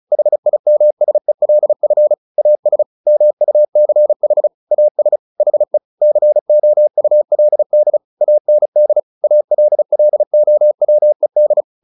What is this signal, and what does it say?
Morse code, heard as a keyed tone, about 36 wpm, 600 Hz. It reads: HIMSELF AS MUCH AS HE COULD AND ALLOWED
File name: the old man and the sea 2 35wpm